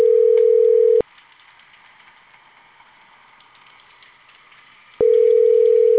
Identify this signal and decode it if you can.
Ringback tone (North America), 440 + 480 Hz.